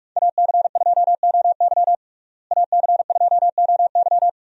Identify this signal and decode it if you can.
Morse code, heard as a keyed tone, about 41 wpm, 700 Hz. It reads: AC2OY AC2OY
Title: USA Call Signs - 5 Characters - Repeated 2x - 45wpm